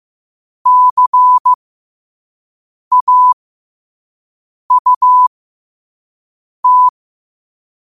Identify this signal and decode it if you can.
Morse code, heard as a keyed tone, about 15 wpm, spaced as for 5 wpm, 1000 Hz. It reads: CAUT
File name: qotdfiles / 5wpm